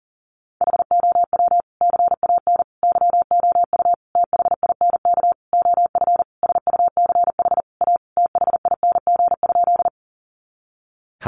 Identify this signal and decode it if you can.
Morse code, heard as a keyed tone, about 37 wpm, 700 Hz. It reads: HOW CAN YOU THINK OF SUCH A THING?
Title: Sentences from Top 200 Words 40wpm